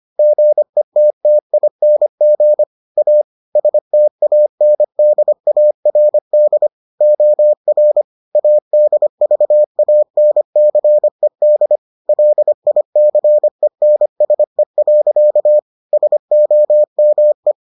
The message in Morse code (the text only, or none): GETTING A STANDARD OR ADVANCED LICENSE. SOME